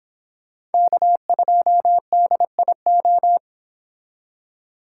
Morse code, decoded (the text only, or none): K2DIO